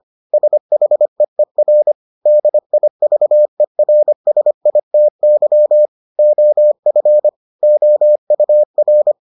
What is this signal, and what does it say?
Morse code, heard as a keyed tone, about 25 wpm, 600 Hz. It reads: SHEER DIVERSITY OF OUR